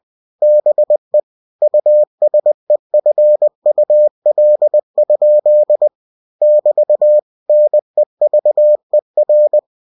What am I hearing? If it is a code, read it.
Morse code, heard as a keyed tone, about 20 wpm, 600 Hz. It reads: BE USEFUL? <BT> NEVER
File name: cw-books / winnie-the-pooh / winnie-20wpm-page-0063